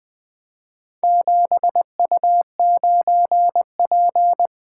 Morse code, heard as a keyed tone, about 20 wpm, 700 Hz. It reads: 7U9P
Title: Four Letter-Number Combinations - 20wpm